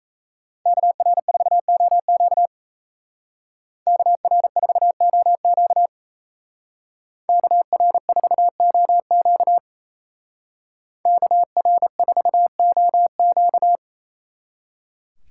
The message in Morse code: KR4OQ KR4OQ KR4OQ KR4OQ

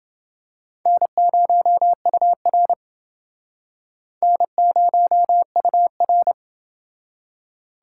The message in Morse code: N0UR N0UR

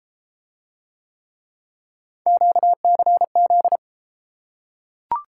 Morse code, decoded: QCZ

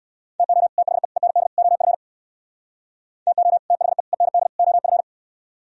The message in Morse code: TO THE END OF TO THE END OF

